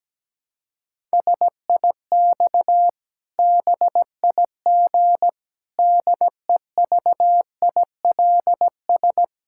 Morse code, heard as a keyed tone, about 17 wpm, 700 Hz. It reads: SIX BIG DEVILS
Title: Pangram Challenge v2 - 17wpm